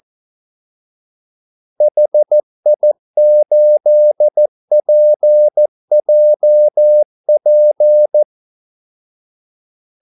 Morse code, heard as a keyed tone, about 14 wpm, 600 Hz. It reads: HI8PJP
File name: Callsigns-14wpm